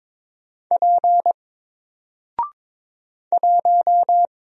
Morse code, heard as a keyed tone, about 22 wpm, 700 Hz. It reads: P 1
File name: Single-Letter-Number-22wpm